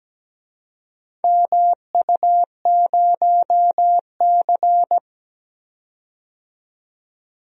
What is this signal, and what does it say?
Morse code, heard as a keyed tone, about 17 wpm, 700 Hz. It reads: MU0C
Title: DXCC-100 Call Signs - 4 Characters - 17wpm